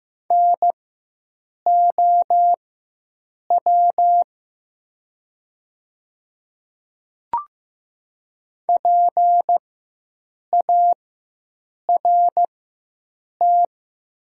Morse code, heard as a keyed tone, about 15 wpm, spaced as for 7 wpm, 700 Hz. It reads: NOW PART
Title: 4x Character Spacing - Top 300 Words - 15wpm